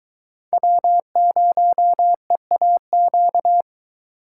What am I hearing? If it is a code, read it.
Morse code, heard as a keyed tone, about 23 wpm, 700 Hz. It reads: W0EAQ